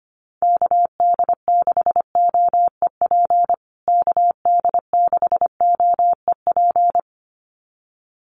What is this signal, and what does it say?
Morse code, heard as a keyed tone, about 25 wpm, 700 Hz. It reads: KD6OEP KD6OEP
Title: USA Call Signs Repeated 2x 25wpm